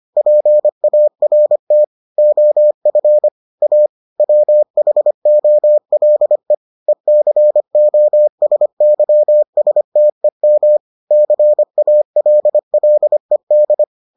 Morse code, heard as a keyed tone, about 25 wpm, 600 Hz. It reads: PART OF A WHOLE ECOSYSTEM CALLED